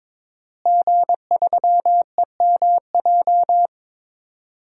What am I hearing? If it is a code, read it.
Morse code, heard as a keyed tone, about 22 wpm, 700 Hz. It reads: G3EMJ
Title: DXCC-100 Call Signs - 5 Characters - Repeated 3x - 22wpm